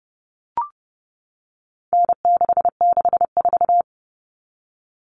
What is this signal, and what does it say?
Morse code, heard as a keyed tone, about 30 wpm, 700 Hz. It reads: N664
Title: All Cut Numbers - 4-Digit 30wpm